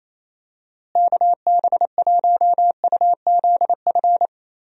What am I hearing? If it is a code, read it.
Morse code, heard as a keyed tone, about 28 wpm, 700 Hz. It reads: KB1UZF
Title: USA Call Signs - 6 Characters - Repeated 2x - 28wpm